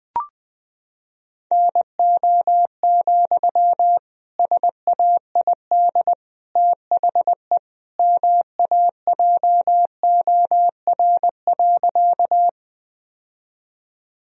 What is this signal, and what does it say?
Morse code, heard as a keyed tone, about 20 wpm, 700 Hz. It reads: NO, SAID THE MAJOR.
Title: Sentences from Top 1000 Words - Review Part 4 - 20wpm